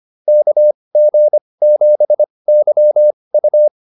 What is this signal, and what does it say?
Morse code, heard as a keyed tone, about 25 wpm, spaced as for 20 wpm, 600 Hz. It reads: KG7YU